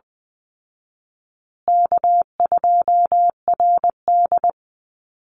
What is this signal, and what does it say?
Morse code, heard as a keyed tone, about 20 wpm, 700 Hz. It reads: K2RD
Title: Easy Call Signs Repeated 3x - 20wpm